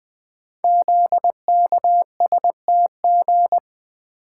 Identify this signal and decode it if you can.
Morse code, heard as a keyed tone, about 20 wpm, 700 Hz. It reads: ZKSTG
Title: Code Groups - 5 Letters - 20wpm